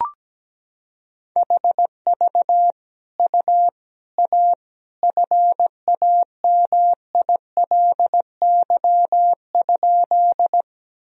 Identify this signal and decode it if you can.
Morse code, heard as a keyed tone, about 17 wpm, 700 Hz. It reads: HV U A FAMILY?
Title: Sentences from Top 400 Words and Abbreviations 17wpm